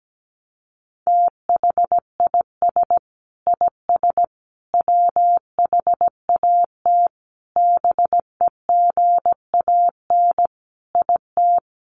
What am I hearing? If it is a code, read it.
Morse code, heard as a keyed tone, about 17 wpm, 700 Hz. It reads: THIS IS WHAT BEGAN IT
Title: Sentences from Top 300 Words 17wpm